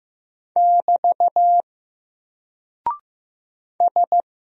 Morse code, heard as a keyed tone, about 15 wpm, 700 Hz. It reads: <BT> S
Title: Single Character (Letter:Number:Punctuation:Prosign) 15wpm